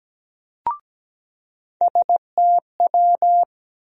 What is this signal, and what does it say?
Morse code, heard as a keyed tone, about 17 wpm, 700 Hz. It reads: STW
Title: USA Call Sign Postfixes - 3 Letters 17wpm